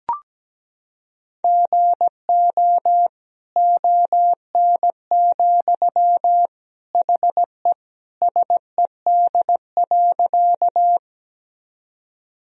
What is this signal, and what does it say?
Morse code, heard as a keyed tone, about 17 wpm, 700 Hz. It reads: GO ON, HE SED.
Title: Sentences from Top 100 Words and Abbreviations 17wpm